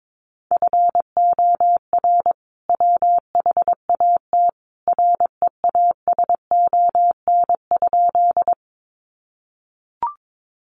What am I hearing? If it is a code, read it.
Morse code, heard as a keyed tone, about 22 wpm, 700 Hz. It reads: FOR WHAT REASON?
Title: Sentences from Top 700 Words 22wpm